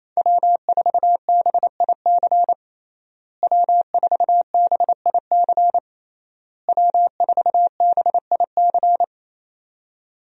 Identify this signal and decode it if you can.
Morse code, heard as a keyed tone, about 28 wpm, 700 Hz. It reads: W4BIC W4BIC W4BIC
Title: USA Call Signs - 5 Characters - Repeated 3x - 28wpm